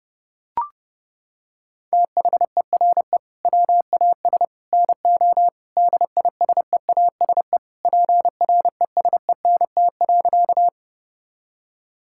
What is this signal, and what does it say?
Morse code, heard as a keyed tone, about 32 wpm, 700 Hz. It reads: THERE WAS NO DISEASE PRESENT.
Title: Sentences from Top 1000 Words - Review Part 6 - 30wpm